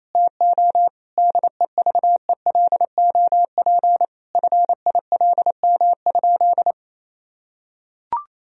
Morse code, heard as a keyed tone, about 28 wpm, 700 Hz. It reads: TO DEVELOP FILM?
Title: Sentences from Top 1000 Words - Review Part 3 - 28wpm